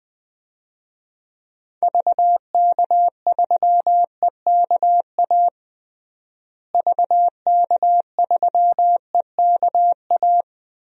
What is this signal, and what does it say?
Morse code, heard as a keyed tone, about 20 wpm, 700 Hz. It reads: VK3EKA VK3EKA
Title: DXCC-100 Call Signs - 6 Characters - Repeated 2x - 20wpm